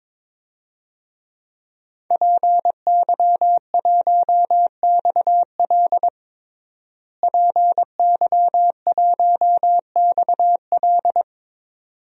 Morse code, heard as a keyed tone, about 22 wpm, 700 Hz. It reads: PY1XL PY1XL
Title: DXCC-100 Call Signs - 5 Characters - Repeated 2x - 22wpm